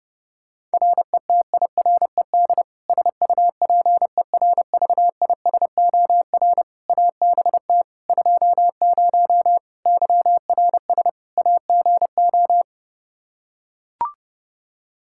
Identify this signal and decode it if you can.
Morse code, heard as a keyed tone, about 30 wpm, 700 Hz. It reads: RETIRED SUPERVISOR ABT 20 YRS AGO
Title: Practice QSO - Exchange 3 - More Info - 30wpm